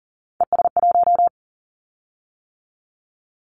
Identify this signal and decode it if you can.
Morse code, heard as a keyed tone, about 39 wpm, 700 Hz. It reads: ES1
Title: Callsign Prefixes for Top 100 DXCC - 40wpm